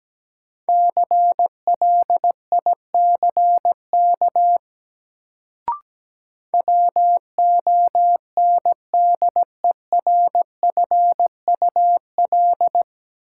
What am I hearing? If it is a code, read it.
Morse code, heard as a keyed tone, about 17 wpm, 700 Hz. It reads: CLICK WONDERFUL
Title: Sentences from Top 1100 Words - New Word Drill - 17wpm